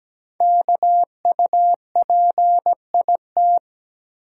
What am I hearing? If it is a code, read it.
Morse code, heard as a keyed tone, about 17 wpm, 700 Hz. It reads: KUPIT
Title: Code Groups - 5 Letter:Numbers - 17wpm